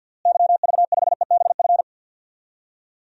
Morse code, heard as a keyed tone, about 50 wpm, 700 Hz. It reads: YV5EBF
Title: DXCC-100 Call Signs - 6 Characters - Repeated 2x - 50wpm